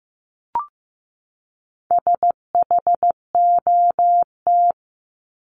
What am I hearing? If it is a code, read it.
Morse code, heard as a keyed tone, about 15 wpm, 700 Hz. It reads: SHOT
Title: Top 1001-1250 Words 15wpm